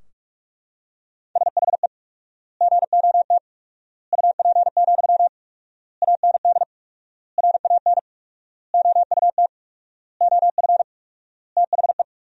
Morse code, heard as a keyed tone, about 41 wpm, 700 Hz. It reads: SHE GOT UP, AND RAN OUT OF THE
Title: 4x Word-Spacing - Sentences from Top 500 Words - 45wpm